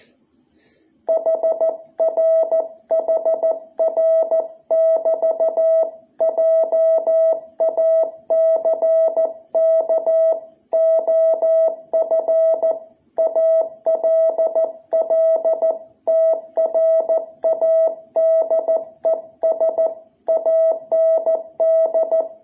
Morse code, heard as a keyed tone, about 14 wpm, 660 Hz. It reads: HR HR <BT> JACK OF ALL TRADES AND
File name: jh8cbh / 2021cw